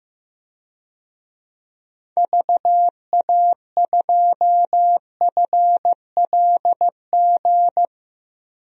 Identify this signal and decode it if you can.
Morse code, heard as a keyed tone, about 15 wpm, 700 Hz. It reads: VA2FLG
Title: DXCC-100 Call Signs - 6 Characters - 15wpm